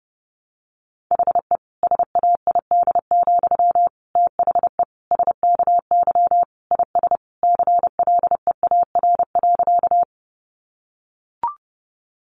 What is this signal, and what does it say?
Morse code, heard as a keyed tone, about 30 wpm, 700 Hz. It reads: HE SAID, THE SKY IS CLEAR.
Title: Sentences from Top 1200 Words 30wpm